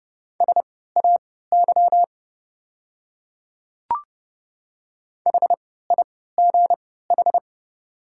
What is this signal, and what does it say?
Morse code, heard as a keyed tone, about 32 wpm, spaced as for 17 wpm, 690 Hz. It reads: SAY HIGH
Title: 3x Character Spacing - Top 300 Words - 30wpm